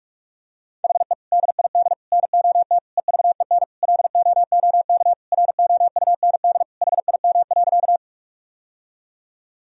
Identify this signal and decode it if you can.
Morse code, heard as a keyed tone, about 46 wpm, 700 Hz. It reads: HE DID NOT EVEN LOOK ROUND HIM.
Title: Sentences from Top 1000 Words - Review Part 1 - 45wpm